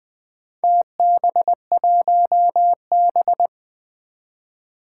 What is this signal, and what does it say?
Morse code, heard as a keyed tone, about 20 wpm, 700 Hz. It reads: TB1B
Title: Four Challenging Letter-Number Combinations 20wpm